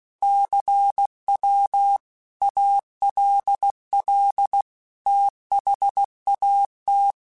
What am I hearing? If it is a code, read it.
Morse code, heard as a keyed tone, about 16 wpm, 800 Hz. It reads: CW ALL THAT
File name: QSOs / QSOs-mp3s / WD0BGP 2@15